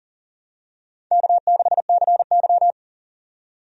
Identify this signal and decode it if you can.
Morse code, heard as a keyed tone, about 40 wpm, 700 Hz. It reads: K6CY
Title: USA Call Signs - 4 Characters - 40wpm